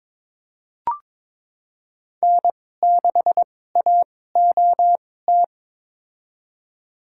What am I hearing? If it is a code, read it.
Morse code, heard as a keyed tone, about 21 wpm, spaced as for 16 wpm, 700 Hz. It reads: N6AOT
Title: USA Call Signs - 5 Characters - 2x Character Spacing - 22wpm